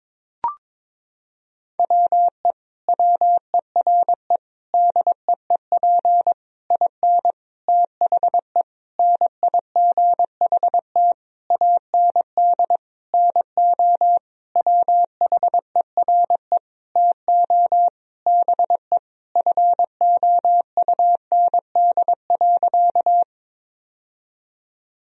Morse code, heard as a keyed tone, about 22 wpm, 700 Hz. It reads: WE WERE DEEP IN THE NIGHT AND NO WHERE TO BE FOUND.
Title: Sentences from Top 1000 Words - Review Part 3 - 22wpm